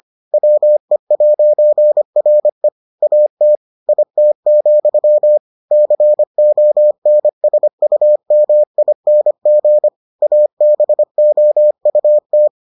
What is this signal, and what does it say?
Morse code, heard as a keyed tone, about 25 wpm, 600 Hz. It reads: WE'RE AT IT, CONSUMING ABOUT